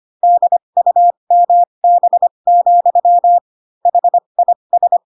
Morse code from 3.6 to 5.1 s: HIS